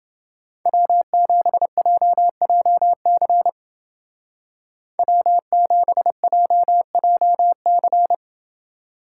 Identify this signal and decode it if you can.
Morse code, heard as a keyed tone, about 29 wpm, 700 Hz. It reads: W7JJC W7JJC